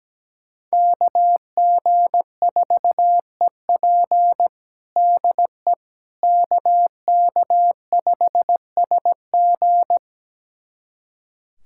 Morse code, heard as a keyed tone, about 17 wpm, 700 Hz. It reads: KG4EP DE KK5SG
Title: USA Callsign DE Callsign - 5 Characters - 17wpm